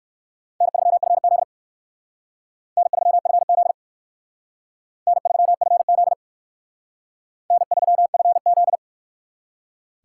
Morse code, heard as a keyed tone, about 59 wpm, 700 Hz. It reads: N3FZ N3FZ N3FZ N3FZ